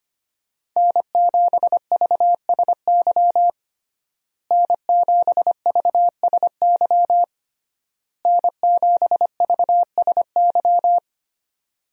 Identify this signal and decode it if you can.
Morse code, heard as a keyed tone, about 25 wpm, 700 Hz. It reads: N7VSY N7VSY N7VSY
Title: USA Call Signs - 5 Characters - Repeated 3x - 25wpm